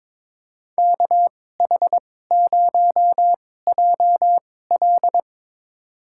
Morse code, heard as a keyed tone, about 22 wpm, spaced as for 16 wpm, 700 Hz. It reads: KH0JL
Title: USA Call Signs - 5 Characters - 2x Character Spacing - 22wpm